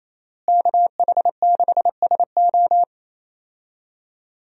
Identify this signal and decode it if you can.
Morse code, heard as a keyed tone, about 27 wpm, 700 Hz. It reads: KH6SO
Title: USA Call Signs - 5 Characters - Repeated 3x - 28wpm